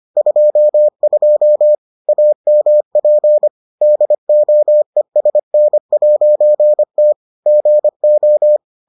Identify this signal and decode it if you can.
Morse code, heard as a keyed tone, about 25 wpm, 600 Hz. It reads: 22 AMP DOESN'T GO